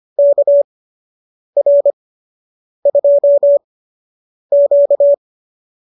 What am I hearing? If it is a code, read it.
Morse code, heard as a keyed tone, about 25 wpm, spaced as for 8 wpm, 590 Hz. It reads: KR2Q